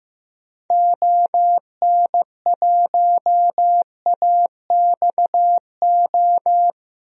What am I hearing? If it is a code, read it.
Morse code, heard as a keyed tone, about 15 wpm, 700 Hz. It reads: ON1AXO
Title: DXCC-100 Call Signs - 6 Characters - 15wpm